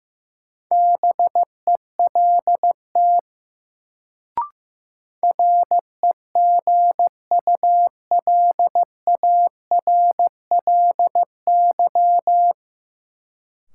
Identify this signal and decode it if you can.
Morse code, heard as a keyed tone, about 15 wpm, 700 Hz. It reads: BELT REGULARLY
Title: Sentences from Top 1400 Words - New Word Drill - 15wpm